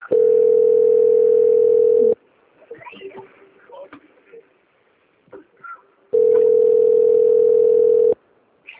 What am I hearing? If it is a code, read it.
Ringback tone (North America), 440 + 480 Hz, with speech.
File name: out- - - - - .46357